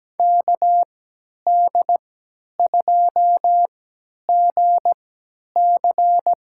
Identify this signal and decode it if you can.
Morse code, heard as a keyed tone, about 17 wpm, spaced as for 10 wpm, 700 Hz. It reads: KD2GC
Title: USA Call Signs - 6 Characters - 3x Character Spacing - 17wpm